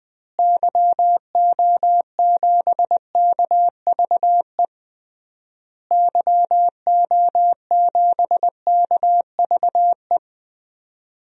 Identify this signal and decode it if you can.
Morse code, heard as a keyed tone, about 20 wpm, 700 Hz. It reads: YO7KVE YO7KVE
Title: DXCC-100 Call Signs - 6 Characters - Repeated 3x - 20wpm